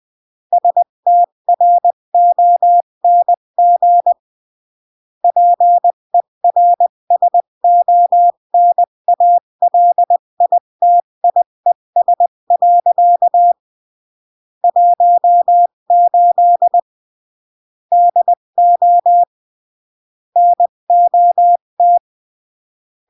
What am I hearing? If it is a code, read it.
Morse code, heard as a keyed tone, about 20 wpm, 700 Hz. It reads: STRONG PERSONALITIES. 18 DO NOT
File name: CW / jodorowsky / 82 commandments 20-15